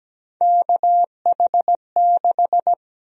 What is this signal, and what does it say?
Morse code, heard as a keyed tone, about 17 wpm, 700 Hz. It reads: KH6
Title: USA Call Sign Prefixes - Repeated 2x - 17wpm